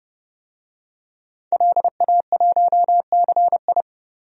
Morse code, heard as a keyed tone, about 30 wpm, 700 Hz. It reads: LA1CI
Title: DXCC-100 Call Signs - 5 Characters - 30wpm